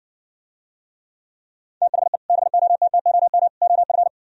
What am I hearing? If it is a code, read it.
Morse code, heard as a keyed tone, about 60 wpm, 700 Hz. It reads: THE BOTTOM OF